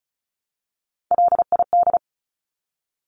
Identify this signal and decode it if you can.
Morse code, heard as a keyed tone, about 34 wpm, 700 Hz. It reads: LID
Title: Abbreviations Repeated 3x - 35wpm